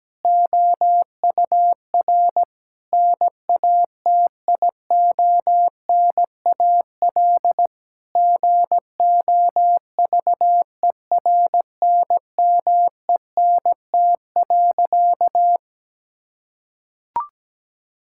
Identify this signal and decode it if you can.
Morse code, heard as a keyed tone, about 17 wpm, 700 Hz. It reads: OUR NATIONAL GOVERNMENT.
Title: Sentences from Top 1000 Words - Review Part 6 - 17wpm